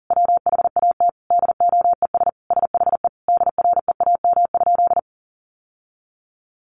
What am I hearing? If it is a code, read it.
Morse code, heard as a keyed tone, about 40 wpm, 700 Hz. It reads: WHAT DOES SHE DREAM?
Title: Sentences from Top 1500 Words 40wpm